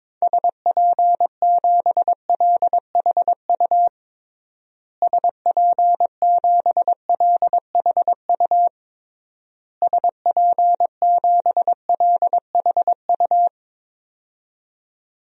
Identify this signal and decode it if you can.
Morse code, heard as a keyed tone, about 22 wpm, 700 Hz. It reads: SP7LHU SP7LHU SP7LHU
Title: DXCC-100 Call Signs - 6 Characters - Repeated 3x - 22wpm